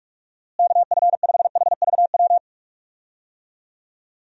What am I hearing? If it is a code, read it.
Morse code, heard as a keyed tone, about 43 wpm, 700 Hz. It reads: KF5HVW
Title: USA Call Signs - 6 Characters - Repeated 3x - 45wpm